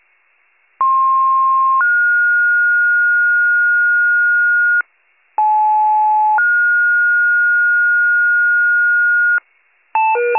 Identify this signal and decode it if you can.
Two-tone page, heard as a one-second tone, then a three-second tone, about 1060 then 1430 Hz, about 860 then 1430 Hz. It is heard from 0.8 to 9.4 s.